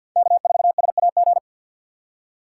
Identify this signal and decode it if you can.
Morse code, heard as a keyed tone, about 52 wpm, 690 Hz. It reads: K4SAG
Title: USA Call Signs - 5 Characters - 3x Character Spacing - 50wpm